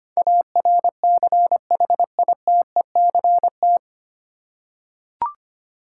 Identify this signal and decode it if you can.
Morse code, heard as a keyed tone, about 25 wpm, 700 Hz. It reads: ARCHITECT
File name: practice / 25wpm / 9 Letter Words - First 3 Letters Given - 25wpm